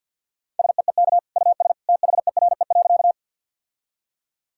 Morse code, heard as a keyed tone, about 50 wpm, 700 Hz. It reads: SEEK US THERE.
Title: Sentences from Top 1600 Words 50wpm